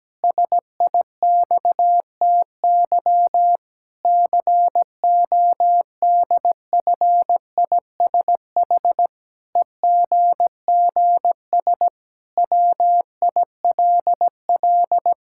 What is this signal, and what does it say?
Morse code, heard as a keyed tone, about 17 wpm, 700 Hz. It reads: SIXTY CODFISH EGGS WILL
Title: Pangram Challenge v2 - 17wpm